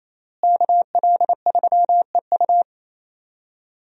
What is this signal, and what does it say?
Morse code, heard as a keyed tone, about 28 wpm, 700 Hz. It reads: KL3EU
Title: USA Call Signs - 5 Characters - Repeated 3x - 28wpm